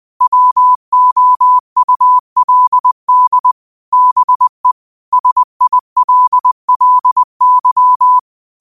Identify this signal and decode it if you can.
Morse code, heard as a keyed tone, about 20 wpm, 1000 Hz. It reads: WOULD BE SILLY
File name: qotdfiles / 20wpm